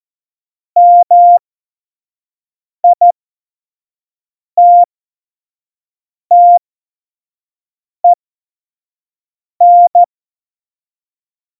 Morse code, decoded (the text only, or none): MITTEN